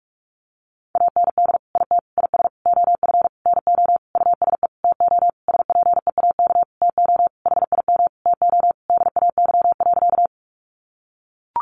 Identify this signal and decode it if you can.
Morse code, heard as a keyed tone, about 44 wpm, 700 Hz. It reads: AND IT IS OF NO USE TO SPEAK TO HIM TO DAY.
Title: Sentences from Top 1100 Words 45wpm